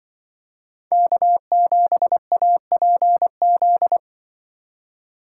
Morse code, heard as a keyed tone, about 24 wpm, 700 Hz. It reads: K7APZ
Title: Speed Racing - USA 1x3 Call Signs 20wpm